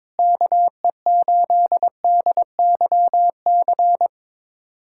Morse code, heard as a keyed tone, about 22 wpm, 700 Hz. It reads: KE8DYC
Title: USA Callsign DE Callsign - 6 Characters - 22wpm